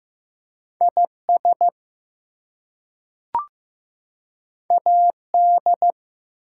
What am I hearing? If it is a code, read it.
Morse code, heard as a keyed tone, about 15 wpm, 700 Hz. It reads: IS AD
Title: Two Letter Words - 15wpm